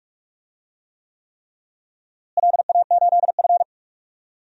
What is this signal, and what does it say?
Morse code, heard as a keyed tone, about 46 wpm, 700 Hz. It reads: LA8F
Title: DXCC-100 Call Signs - 4 Characters - Repeated 3x - 45wpm